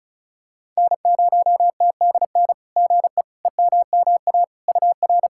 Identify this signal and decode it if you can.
Morse code, heard as a keyed tone, about 36 wpm, 690 Hz. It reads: N0TDN GE EMMA UR
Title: SOTA QSO Practice 35wpm